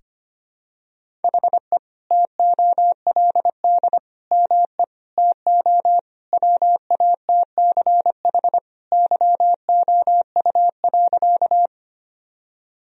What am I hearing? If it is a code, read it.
Morse code, heard as a keyed tone, about 25 wpm, 700 Hz. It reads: HE TOLD ME TO WATCH YOU.
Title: Sentences from Top 1000 Words - Review Part 1 - 25wpm